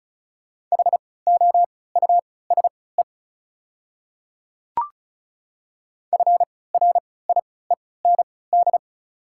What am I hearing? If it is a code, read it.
Morse code, heard as a keyed tone, about 34 wpm, spaced as for 20 wpm, 690 Hz. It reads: HOUSE FRIEND
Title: 3x Character Spacing - Top 300 Words - 35wpm